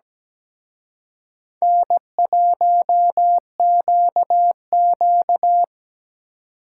Morse code, heard as a keyed tone, about 20 wpm, 700 Hz. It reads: N1QQ
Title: Speed Racing - USA 1x2 Call Signs 17wpm